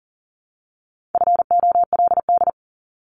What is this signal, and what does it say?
Morse code, heard as a keyed tone, about 40 wpm, 700 Hz. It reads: FOLD
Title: Top 1751-2000 Words 40wpm